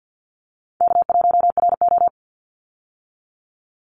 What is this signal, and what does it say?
Morse code, heard as a keyed tone, about 50 wpm, 700 Hz. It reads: K1RO
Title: Easy Call Signs Repeated 2x - 50wpm